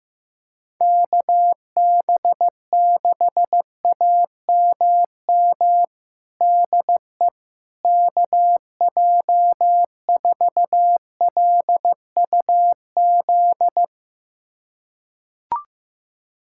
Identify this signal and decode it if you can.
Morse code, heard as a keyed tone, about 15 wpm, 700 Hz. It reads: KB6AMM DE KJ4LUZ
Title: USA Callsign DE Callsign - 6 Characters - 15wpm